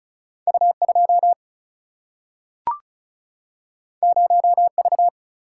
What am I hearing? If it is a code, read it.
Morse code, heard as a keyed tone, about 36 wpm, 700 Hz. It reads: U2 0V
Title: Single-Letter Single-Number 35wpm